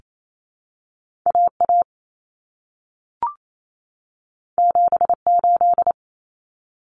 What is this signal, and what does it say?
Morse code, heard as a keyed tone, about 28 wpm, 700 Hz. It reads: AA 78
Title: Common Cut Numbers - 2-Digit 28wpm